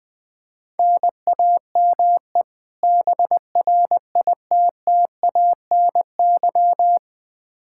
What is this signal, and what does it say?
Morse code, heard as a keyed tone, about 20 wpm, 700 Hz. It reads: NAME BRITTANY
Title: K1USN SST QSO Practice 20wpm